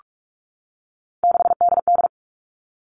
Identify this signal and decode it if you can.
Morse code, heard as a keyed tone, about 46 wpm, 700 Hz. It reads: 6DD
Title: Three Challenging Letter-Number Combinations 45wpm